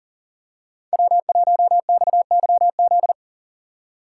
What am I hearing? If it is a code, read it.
Morse code, heard as a keyed tone, about 40 wpm, 700 Hz. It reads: W1XYZ
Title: USA Call Signs - 5 Characters - 3x Character Spacing - 40wpm